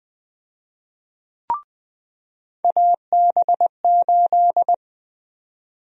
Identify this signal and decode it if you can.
Morse code, heard as a keyed tone, about 20 wpm, 700 Hz. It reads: AB8
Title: USA Call Sign Prefixes - 20wpm